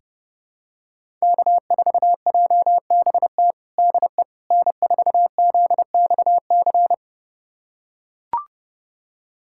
Morse code, heard as a keyed tone, about 30 wpm, 700 Hz. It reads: K4JBT DE N4ZXC
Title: USA Callsign DE Callsign - 5 Characters - 30wpm